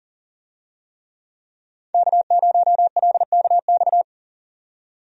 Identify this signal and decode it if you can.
Morse code, heard as a keyed tone, about 40 wpm, 700 Hz. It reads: K0LKX
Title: USA Call Signs - 5 Characters - 40wpm